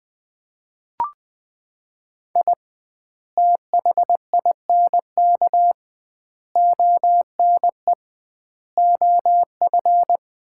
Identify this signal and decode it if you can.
Morse code, heard as a keyed tone, about 20 wpm, 700 Hz. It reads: I THINK ONE OF
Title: 2x Word-Spacing - Sets of 5 Words - 20wpm